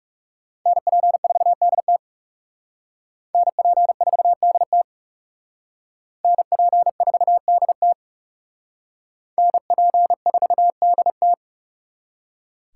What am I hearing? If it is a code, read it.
Morse code, heard as a keyed tone, about 37 wpm, 700 Hz. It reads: NP4DT NP4DT NP4DT NP4DT
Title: Speed Racing - USA 2x2 Call Signs 30wpm